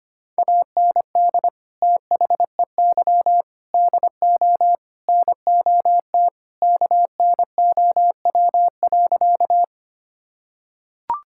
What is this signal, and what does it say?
Morse code, heard as a keyed tone, about 25 wpm, 700 Hz. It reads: AND THEY DO NOT KNOW.
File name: practice / 25wpm / Sentences with 4-Word Phrases - 25wpm